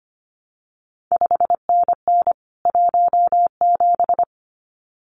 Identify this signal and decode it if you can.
Morse code, heard as a keyed tone, about 25 wpm, 700 Hz. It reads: 5NN 17
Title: CQ WW DX Contest Exchange Encore - 25wpm